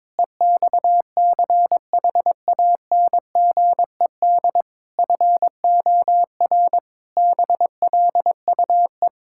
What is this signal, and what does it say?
Morse code, heard as a keyed tone, about 22 wpm, 700 Hz. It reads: EXCHANGED FOR BLUE
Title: Pangram Challenge v2 - 22wpm